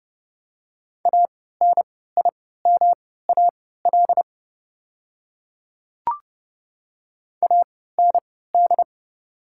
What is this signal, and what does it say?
Morse code, heard as a keyed tone, about 30 wpm, spaced as for 17 wpm, 690 Hz. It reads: ANIMAL AND
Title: 3x Character Spacing - Top 200 Words - 30wpm